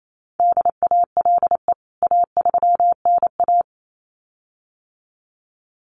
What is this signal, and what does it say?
Morse code, heard as a keyed tone, about 28 wpm, 700 Hz. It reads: DALE A3NA
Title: CWT Exchange 28wpm